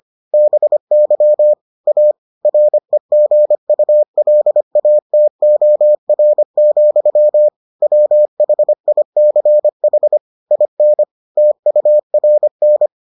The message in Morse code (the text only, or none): BY A REGULATOR, WHICH IN TURN